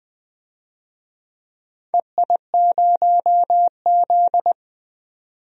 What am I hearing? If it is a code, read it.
Morse code, heard as a keyed tone, about 20 wpm, 700 Hz. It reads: EI0Z
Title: DXCC-100 Call Signs - 4 Characters - 20wpm